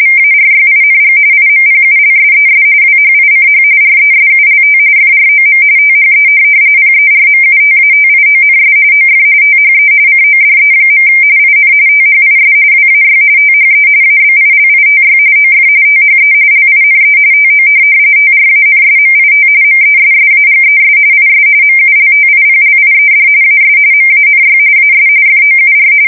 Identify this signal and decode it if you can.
RTTY, about 45.45 baud. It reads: VARYING IMPULSES WHICH RESULT FROM THE PERMUTATIONS OF THE CODE. IN THIS WAY THE ABILITY OF THE LINE TO TRANSMIT MESSAGES IS MULTIPLIED AS MANY AS FOUR
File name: fsk170 / printing-telegraph